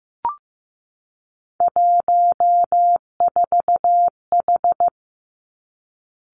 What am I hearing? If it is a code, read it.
Morse code, heard as a keyed tone, about 15 wpm, 700 Hz. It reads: 14H
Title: Three Challenging Letter-Number Combinations 15wpm